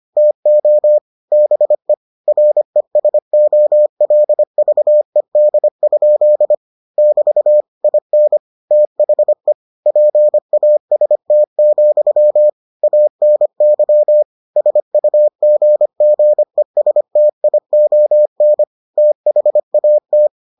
Morse code, heard as a keyed tone, about 25 wpm, 600 Hz. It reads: TO BE RESOLVED? <BT> IN THE PAST, ANY SUGGESTION THAT